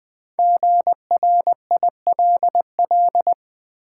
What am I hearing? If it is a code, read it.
Morse code, heard as a keyed tone, about 20 wpm, 700 Hz. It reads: GRILL
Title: Easy Words with Hints - 20wpm